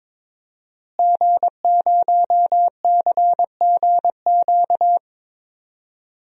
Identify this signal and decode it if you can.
Morse code, heard as a keyed tone, about 22 wpm, 700 Hz. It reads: G0CGQ
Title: DXCC-100 Call Signs - 5 Characters - Repeated 3x - 22wpm